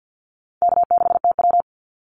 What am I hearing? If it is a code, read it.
Morse code, heard as a keyed tone, about 48 wpm, 700 Hz. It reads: K6TW
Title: NA SOTA Activator Callsigns 50wpm